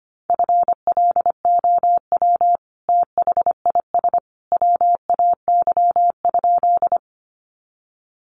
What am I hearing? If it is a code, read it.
Morse code, heard as a keyed tone, about 25 wpm, 700 Hz. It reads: FLOW THIS WAY?
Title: Sentences from Top 1200 Words 25wpm